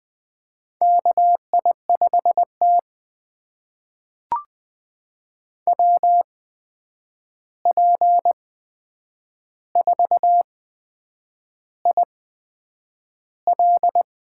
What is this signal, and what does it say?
Morse code, heard as a keyed tone, about 20 wpm, spaced as for 8 wpm, 700 Hz. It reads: KI5T WP4IL
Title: USA Call Signs - 8x Character Spacing - 20wpm